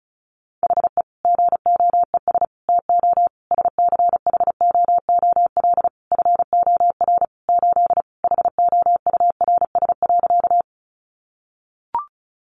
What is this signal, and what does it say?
Morse code, heard as a keyed tone, about 34 wpm, 700 Hz. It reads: HE GOES TO SCHOOL FOR 8 HOURS.
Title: Additional Easy Phrases 35wpm